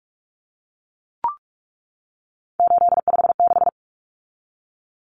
Morse code, heard as a keyed tone, about 44 wpm, 700 Hz. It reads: 856
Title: Numbers - 3-Digit 45wpm